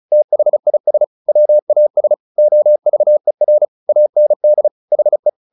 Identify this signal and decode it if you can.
Morse code, heard as a keyed tone, about 34 wpm, 600 Hz. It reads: THIS WAS OVER AND HE
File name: the old man and the sea / the old man and the sea 1 35wpm